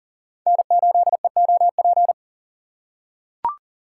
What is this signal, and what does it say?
Morse code, heard as a keyed tone, about 40 wpm, 700 Hz. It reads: N8EOP
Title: USA Call Signs - 5 Characters - 2x Character Spacing - 40wpm